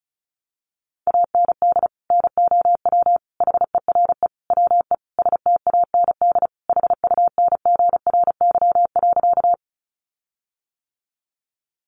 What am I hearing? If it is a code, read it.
Morse code, heard as a keyed tone, about 34 wpm, 700 Hz. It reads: AND NOW HERE WE STAND HUNGRY.
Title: Sentences from Top 1400 Words 35wpm